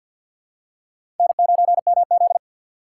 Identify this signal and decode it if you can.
Morse code, heard as a keyed tone, about 52 wpm, 700 Hz. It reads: N9MZ
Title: USA Call Signs - 4 Characters - 2x Character Spacing - 50wpm